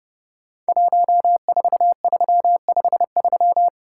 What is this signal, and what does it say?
Morse code, heard as a keyed tone, about 30 wpm, 700 Hz. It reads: 14353
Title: Code Groups - 5 Letter:Numbers - 30wpm